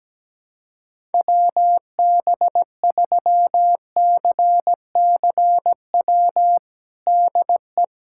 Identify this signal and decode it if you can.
Morse code, heard as a keyed tone, about 17 wpm, 700 Hz. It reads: WB3CCW DE
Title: USA Callsign DE Callsign - 6 Characters - 17wpm